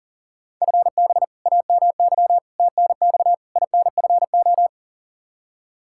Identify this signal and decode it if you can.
Morse code, heard as a keyed tone, about 40 wpm, 700 Hz. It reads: FB AMY TNX INFO
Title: Practice QSO - Exchange 3 - More Info - 40wpm